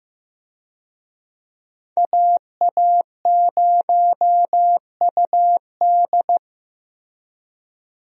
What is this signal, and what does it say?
Morse code, heard as a keyed tone, about 15 wpm, 700 Hz. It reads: AA0UD
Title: USA Call Signs - 5 Characters - Repeated 3x - 15wpm